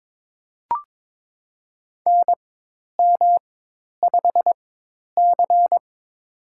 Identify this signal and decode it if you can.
Morse code, heard as a keyed tone, about 22 wpm, spaced as for 10 wpm, 700 Hz. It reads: NM5C
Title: USA Call Signs - 4 Characters - 4x Character Spacing - 22wpm